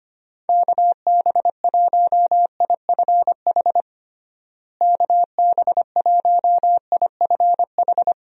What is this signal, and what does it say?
Morse code, heard as a keyed tone, about 25 wpm, 700 Hz. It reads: KB1IFH KB1IFH
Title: USA Call Signs - 6 Characters - Repeated 3x - 25wpm